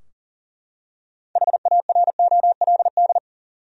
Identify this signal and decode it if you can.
Morse code, heard as a keyed tone, about 40 wpm, 700 Hz. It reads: HAROLD
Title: USA Common Names - 40wpm